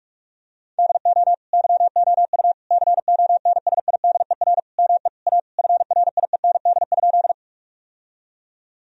Morse code, heard as a keyed tone, about 46 wpm, 700 Hz. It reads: DO YOU CONSIDER ME A FRIEND?
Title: Sentences from Top 1000 Words - Review Part 4 - 45wpm